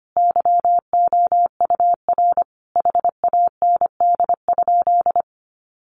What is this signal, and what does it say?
Morse code, heard as a keyed tone, about 25 wpm, 700 Hz. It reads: YOUR HAND?
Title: Sentences from Top 200 Words 25wpm